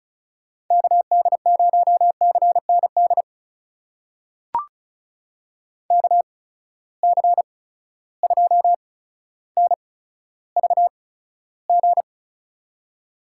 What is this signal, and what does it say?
Morse code, heard as a keyed tone, about 36 wpm, spaced as for 15 wpm, 700 Hz. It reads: KD0CND KC2NVG
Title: USA Call Signs - 6 Characters - 8x Character Spacing - 35wpm